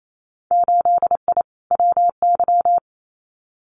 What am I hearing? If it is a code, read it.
Morse code, heard as a keyed tone, about 28 wpm, 700 Hz. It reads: 8I WY
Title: Winter Field Day Exchanges Encore 28wpm